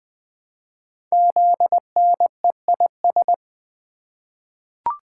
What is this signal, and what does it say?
Morse code, heard as a keyed tone, about 20 wpm, 700 Hz. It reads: ZNEIS
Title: Code Groups - 5 Letters - 20wpm